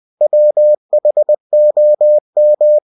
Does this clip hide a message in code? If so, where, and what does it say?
Yes: Morse code from 0.2 to 2.8 s, heard as a keyed tone, about 20 wpm, 600 Hz. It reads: WHOM